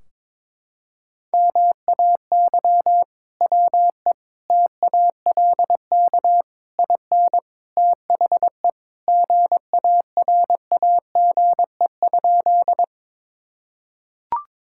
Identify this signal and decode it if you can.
Morse code, heard as a keyed tone, about 22 wpm, 700 Hz. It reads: MAY WE TALK IN THE GARAGE?
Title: Sentences from Top 1600 Words 22wpm